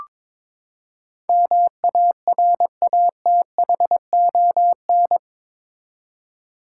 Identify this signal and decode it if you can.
Morse code, heard as a keyed tone, about 21 wpm, 700 Hz. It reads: MARATHON
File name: practice / 22wpm / Eight Letter Words - 22wpm